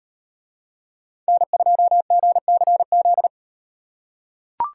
Morse code, heard as a keyed tone, about 38 wpm, 700 Hz. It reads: N2GCZ